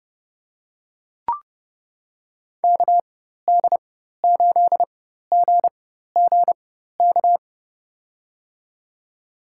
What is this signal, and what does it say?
Morse code, heard as a keyed tone, about 30 wpm, spaced as for 14 wpm, 700 Hz. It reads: KD8GGK
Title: USA Call Signs - 6 Characters - 4x Character Spacing - 30wpm